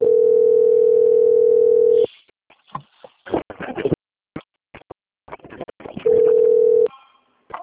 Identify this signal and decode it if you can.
Ringback tone (North America), 440 + 480 Hz.